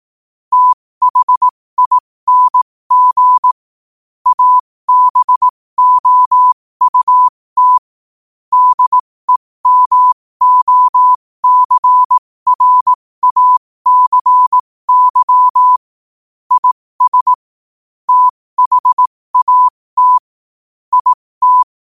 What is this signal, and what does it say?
Morse code, heard as a keyed tone, about 18 wpm, spaced as for 15 wpm, 1000 Hz. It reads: THING ABOUT DEMOCRACY IS THAT IT
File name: qotdfiles / 15wpm